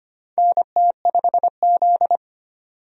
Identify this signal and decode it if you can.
Morse code, heard as a keyed tone, about 25 wpm, 700 Hz. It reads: NT5Z
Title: USA Call Signs - 4 Characters - 25wpm